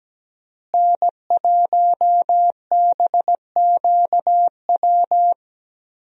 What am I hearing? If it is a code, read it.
Morse code, heard as a keyed tone, about 17 wpm, 700 Hz. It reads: N1BQW
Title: USA Call Signs - 5 Characters - Repeated 2x - 17wpm